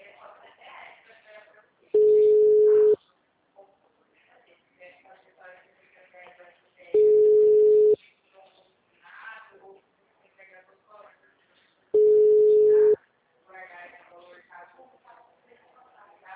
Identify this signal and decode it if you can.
Ringback tone (Europe, 425 Hz), with speech.